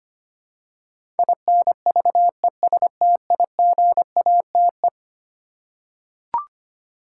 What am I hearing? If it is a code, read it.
Morse code, heard as a keyed tone, about 25 wpm, 700 Hz. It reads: INVESTIGATE
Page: morse-code-ninja-practice-25wpm